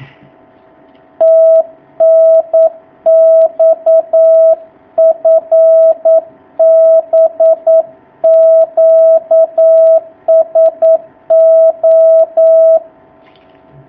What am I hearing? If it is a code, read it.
Morse code, heard as a keyed tone, about 9 wpm, 650 Hz. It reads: TNX FB QSO
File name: jh8cbh / 2019cw